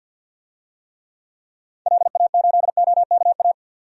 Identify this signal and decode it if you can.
Morse code, heard as a keyed tone, about 48 wpm, 700 Hz. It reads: LA8OKA
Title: DXCC-100 Call Signs - 6 Characters - 50wpm